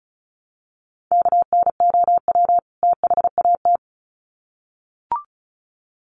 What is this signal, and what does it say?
Morse code, heard as a keyed tone, about 34 wpm, 700 Hz. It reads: KNOW THAT
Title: Sets of 2 Words 35wpm